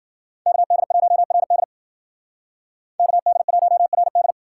Morse code, heard as a keyed tone, about 52 wpm, 700 Hz. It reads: KD1RD KD1RD